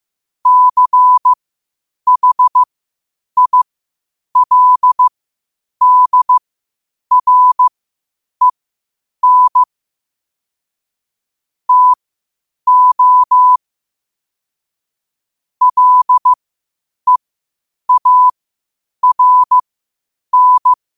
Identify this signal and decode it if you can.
Morse code, heard as a keyed tone, about 15 wpm, spaced as for 8 wpm, 1000 Hz. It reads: CHILDREN TO LEARN
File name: qotdfiles / 8wpm